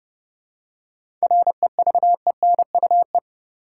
Morse code, heard as a keyed tone, about 29 wpm, 700 Hz. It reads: REVENUE
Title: Seven Letter Words - 30wpm